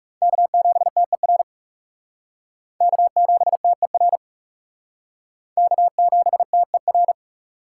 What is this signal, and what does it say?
Morse code, heard as a keyed tone, about 37 wpm, 700 Hz. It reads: K7TER K7TER K7TER
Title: Speed Racing - USA Call Signs 30wpm